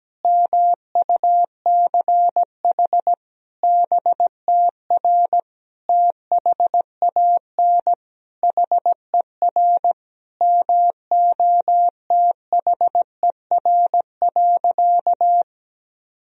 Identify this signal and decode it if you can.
Morse code, heard as a keyed tone, about 17 wpm, 700 Hz. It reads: MUCH BTR THAN HER MOTHER.
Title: Sentences from Top 400 Words and Abbreviations 17wpm